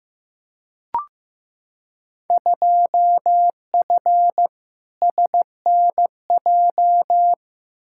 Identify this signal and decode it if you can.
Morse code, heard as a keyed tone, about 15 wpm, 700 Hz. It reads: 2F SNJ
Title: Field Day Exchanges Encore - 15wpm